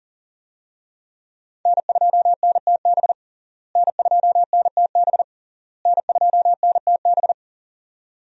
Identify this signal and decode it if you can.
Morse code, heard as a keyed tone, about 40 wpm, 700 Hz. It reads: N2NTB N2NTB N2NTB
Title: USA Call Signs - 5 Characters - Repeated 3x - 40wpm